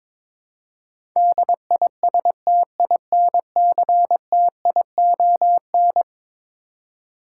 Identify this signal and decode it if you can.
Morse code, heard as a keyed tone, about 22 wpm, 700 Hz. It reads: DISTINCTION
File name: practice / 22wpm / Eleven Letter Words - 22wpm